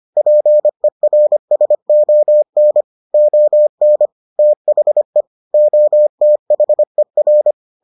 Morse code, heard as a keyed tone, about 25 wpm, 600 Hz. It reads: PERSON ON THE OTHER